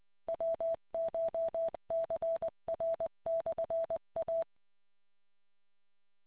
Morse code, heard as a keyed tone, about 20 wpm, 660 Hz. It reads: WCHCR/A